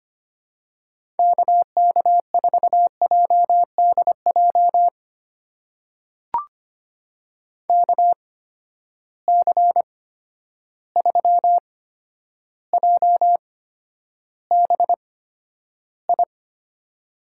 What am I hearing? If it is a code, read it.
Morse code, heard as a keyed tone, about 25 wpm, spaced as for 11 wpm, 700 Hz. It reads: KK4JDJ KC3JBI
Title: USA Call Signs - 6 Characters - 8x Character Spacing - 25wpm